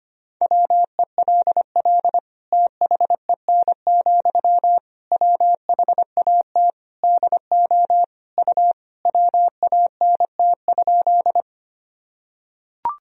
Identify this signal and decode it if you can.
Morse code, heard as a keyed tone, about 25 wpm, 700 Hz. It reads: WELL THEN, WHAT DO U WANT?
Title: Sentences from Top 200 Words and Abbreviations 25wpm